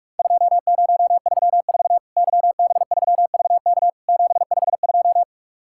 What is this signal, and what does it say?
Morse code, heard as a keyed tone, about 48 wpm, 700 Hz. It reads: 2034 YB3VK 752
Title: Worked All Europe DX Contest - QTC Exchange - 45wpm